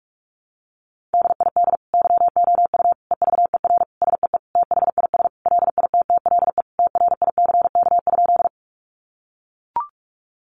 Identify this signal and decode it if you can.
Morse code, heard as a keyed tone, about 44 wpm, 700 Hz. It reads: DID YOU EVER SEE THIS LITTLE TRICK?
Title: Sentences from Top 1500 Words 45wpm